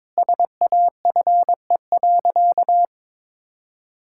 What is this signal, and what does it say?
Morse code, heard as a keyed tone, about 21 wpm, 700 Hz. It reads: SAFE.
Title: Sentences from Top 1600 Words 22wpm